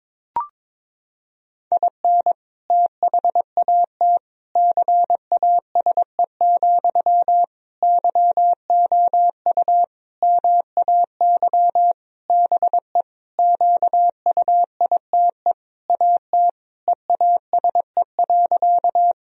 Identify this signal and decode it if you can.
Morse code, heard as a keyed tone, about 22 wpm, 700 Hz. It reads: IN THAT CASE, YOU MAY BE QUITE AT EASE.
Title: Sentences from Top 1000 Words - Review Part 1 - 22wpm